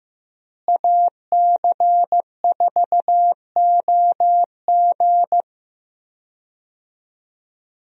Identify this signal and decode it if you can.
Morse code, heard as a keyed tone, about 15 wpm, 700 Hz. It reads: AC4OG
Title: USA Call Signs - 5 Characters - Repeated 3x - 15wpm